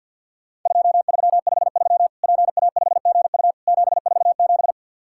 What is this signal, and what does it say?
Morse code, heard as a keyed tone, about 50 wpm, 700 Hz. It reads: 2353 PA5GU 747
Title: Worked All Europe DX Contest - QTC Exchange - 50wpm